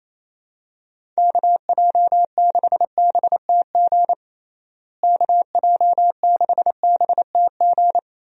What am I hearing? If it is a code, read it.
Morse code, heard as a keyed tone, about 28 wpm, 700 Hz. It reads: KJ6BTG KJ6BTG
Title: USA Call Signs - 6 Characters - Repeated 3x - 28wpm